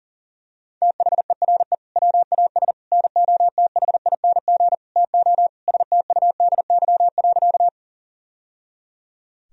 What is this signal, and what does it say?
Morse code, heard as a keyed tone, about 40 wpm, 700 Hz. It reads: THERE WAS NOTHING TO STUDY.
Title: Sentences from Top 1000 Words - Review Part 1 - 40wpm